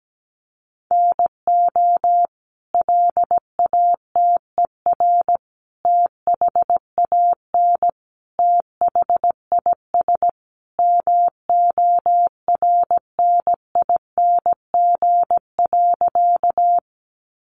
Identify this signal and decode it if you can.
Morse code, heard as a keyed tone, about 17 wpm, 700 Hz. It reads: NO LATER THAN THIS MORNING.
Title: Sentences from Top 700 Words 17wpm